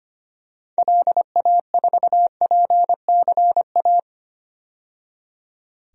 Morse code, heard as a keyed tone, about 25 wpm, 700 Hz. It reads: LA4PCA
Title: DXCC-100 Call Signs - 6 Characters - 25wpm